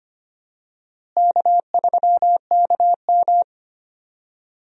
Morse code, heard as a keyed tone, about 25 wpm, 700 Hz. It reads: K3KM
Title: USA Call Signs - 4 Characters - Repeated 3x - 25wpm